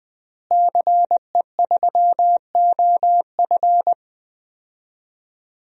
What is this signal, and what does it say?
Morse code, heard as a keyed tone, about 20 wpm, 700 Hz. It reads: CE3OF
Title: DXCC-100 Call Signs - 5 Characters - Repeated 2x - 20wpm